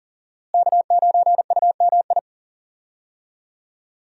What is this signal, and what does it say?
Morse code, heard as a keyed tone, about 40 wpm, 700 Hz. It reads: K9UMI
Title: USA Call Signs - 5 Characters - Repeated 3x - 40wpm